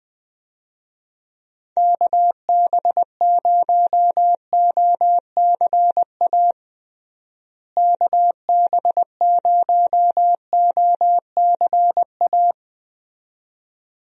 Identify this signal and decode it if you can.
Morse code, heard as a keyed tone, about 20 wpm, 700 Hz. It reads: KB0OCA KB0OCA
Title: USA Call Signs - 6 Characters - Repeated 2x - 20wpm